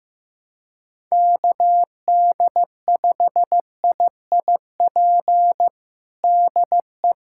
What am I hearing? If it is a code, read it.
Morse code, heard as a keyed tone, about 15 wpm, 700 Hz. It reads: KD5IIP DE
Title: USA Callsign DE Callsign - 6 Characters - 15wpm